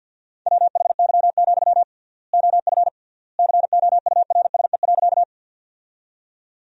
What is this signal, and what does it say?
Morse code, heard as a keyed tone, about 50 wpm, 700 Hz. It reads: WHY, OF COURSE.
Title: 3x Word-Spacing - Sentences from Top 500 Words - 50wpm